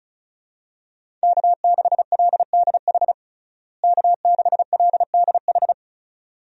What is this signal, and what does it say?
Morse code, heard as a keyed tone, about 36 wpm, 700 Hz. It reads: K6LDH K6LDH
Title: USA Call Signs - 5 Characters - Repeated 2x - 35wpm